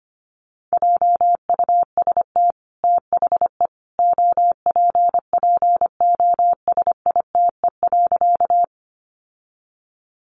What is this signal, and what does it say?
Morse code, heard as a keyed tone, about 25 wpm, 700 Hz. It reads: JUST THE OPPOSITE.
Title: Sentences from Top 1300 Words 25wpm